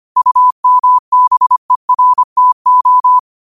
Morse code, heard as a keyed tone, about 25 wpm, 1000 Hz. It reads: UMBERTO
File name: 25wpm